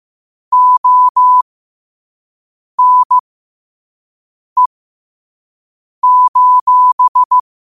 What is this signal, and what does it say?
Morse code, heard as a keyed tone, about 15 wpm, spaced as for 5 wpm, 1000 Hz. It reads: ONE:
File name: qotdfiles / 5wpm